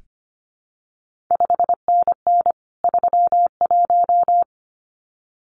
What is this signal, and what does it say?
Morse code, heard as a keyed tone, about 25 wpm, 700 Hz. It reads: 5NN 31
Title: CQ WW DX Contest Exchange Encore - 25wpm